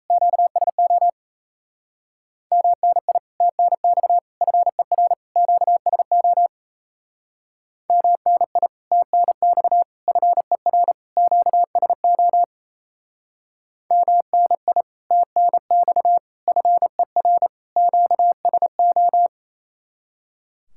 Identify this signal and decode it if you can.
Morse code, heard as a keyed tone, about 35 wpm, 700 Hz. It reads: QSO MNI TNX FER QSO MNI TNX FER QSO MNI TNX FER QSO
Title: Speed-Racing Common QSO Phrases - 28wpm